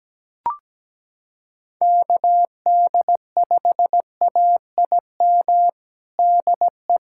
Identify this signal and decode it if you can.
Morse code, heard as a keyed tone, about 17 wpm, 700 Hz. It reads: KD5AIM DE
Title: USA Callsign DE Callsign 17wpm